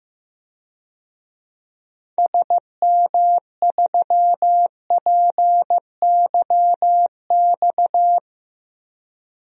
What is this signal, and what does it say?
Morse code, heard as a keyed tone, about 15 wpm, 700 Hz. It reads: SM3PYX
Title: DXCC-100 Call Signs - 6 Characters - 15wpm